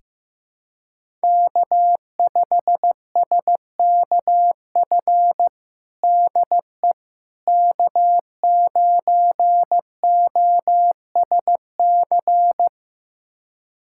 Morse code, heard as a keyed tone, about 16 wpm, 700 Hz. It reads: K5SKF DE K9OSC
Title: USA Callsign DE Callsign - 5 Characters - 15wpm